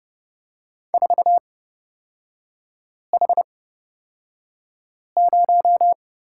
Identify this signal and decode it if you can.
Morse code, heard as a keyed tone, about 30 wpm, spaced as for 5 wpm, 700 Hz. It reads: 4H0
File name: Lesson 38e - Letter q - Three Characters 30wpm